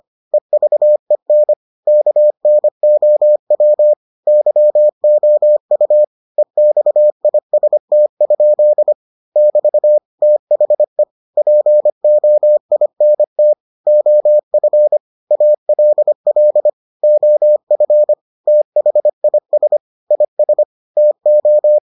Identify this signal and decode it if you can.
Morse code, heard as a keyed tone, about 25 wpm, 600 Hz. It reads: EVEN KNOW YOU EXIST? <BT> THE POINT OF ALL OF THIS IS TO